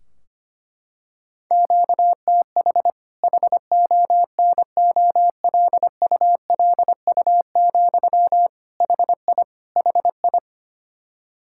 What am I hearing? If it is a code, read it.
Morse code, heard as a keyed tone, about 25 wpm, 700 Hz. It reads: QTH HONOLULU, HI HI
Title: Practice QSO - Exchange 1 - Beginning - 25wpm